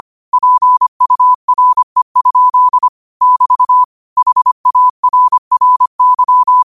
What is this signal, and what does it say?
Morse code, heard as a keyed tone, about 25 wpm, 1000 Hz. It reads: PURE? <BT> HARRY
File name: qotdfiles / 25wpm